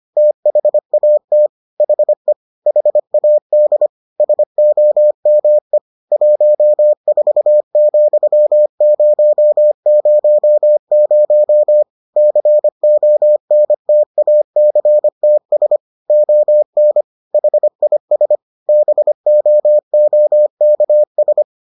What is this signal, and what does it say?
Morse code, heard as a keyed tone, about 25 wpm, 600 Hz. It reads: THAT HE HAD SOME 14,000 CONTACTS ON HIS BOOKS